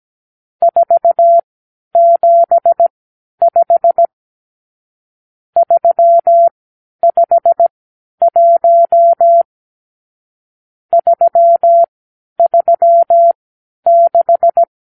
Morse code, heard as a keyed tone, about 17 wpm, spaced as for 10 wpm, 680 Hz. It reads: 475 351 336
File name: k7qo / Lesson 54 (9)